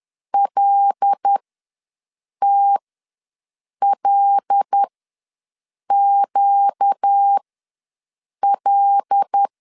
Morse code, heard as a keyed tone, about 10 wpm, spaced as for 6 wpm, 800 Hz. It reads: LTLQL
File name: LEZIONI / 5 LBVC / 5 L ws 10wpm 3636